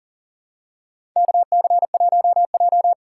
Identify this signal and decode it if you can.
Morse code, heard as a keyed tone, about 38 wpm, 700 Hz. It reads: KC1J
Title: USA Call Signs - 6x Character Spacing - 40wpm